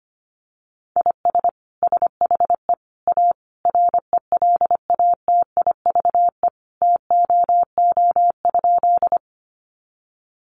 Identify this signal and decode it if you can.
Morse code, heard as a keyed tone, about 25 wpm, 700 Hz. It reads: IS SHE A RELATIVE TOO?
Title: Sentences from Top 1400 Words 25wpm